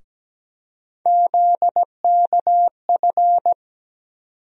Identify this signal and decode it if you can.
Morse code, heard as a keyed tone, about 17 wpm, 700 Hz. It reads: ZKF
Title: USA Call Sign Postfixes - 3 Letters 17wpm